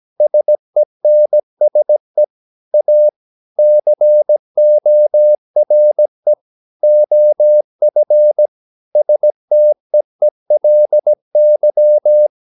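Morse code, heard as a keyed tone, about 17 wpm, 600 Hz. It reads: SENSE A CORE OF STEELY
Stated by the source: 17 wpm Morse practice sending the English text of a novel.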